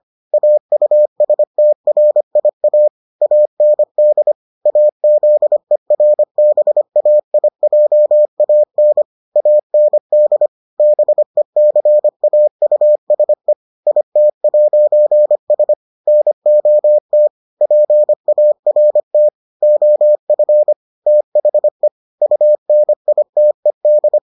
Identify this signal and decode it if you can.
Morse code, heard as a keyed tone, about 25 wpm, 600 Hz. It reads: AUSTRIA AND AZERBAIJAN AND BECAUSE IT'S NOT PART OF THE UNITED